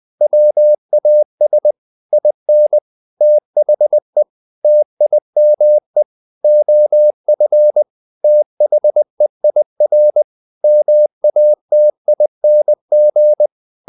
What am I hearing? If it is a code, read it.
Morse code, heard as a keyed tone, about 20 wpm, 600 Hz. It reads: WAS IN THE TIME OF THEIR MATING
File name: the old man and the sea part 2 20wpm